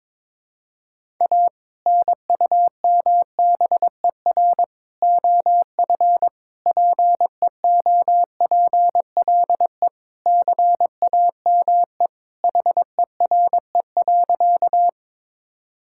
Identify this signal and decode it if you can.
Morse code, heard as a keyed tone, about 22 wpm, 700 Hz. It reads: A NUMBER OF PEOPLE CAME HERE.
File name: practice / 22wpm / Sentences with 2-Word Phrases - 22wpm